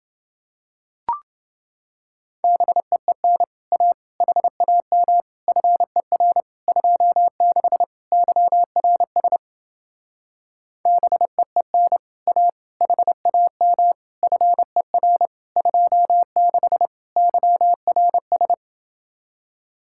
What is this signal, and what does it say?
Morse code, heard as a keyed tone, about 28 wpm, 700 Hz. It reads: BEEN A HAM FER 26 YRS BEEN A HAM FER 26 YRS